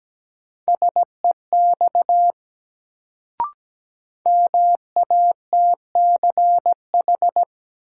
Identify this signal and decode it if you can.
Morse code, heard as a keyed tone, about 17 wpm, 700 Hz. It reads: SEX MATCH
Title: Sentences from Top 1600 Words - New Word Drill Alt - 17wpm